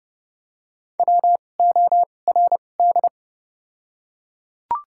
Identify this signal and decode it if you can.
Morse code, heard as a keyed tone, about 30 wpm, spaced as for 22 wpm, 700 Hz. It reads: WORD
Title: 2x Character Spacing - Top 200 Words - 30wpm